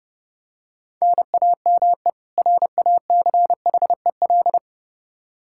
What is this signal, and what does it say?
Morse code, heard as a keyed tone, about 32 wpm, 700 Hz. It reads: NAME RACHEL
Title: Practice QSO - Exchange 1 - Beginning - 30wpm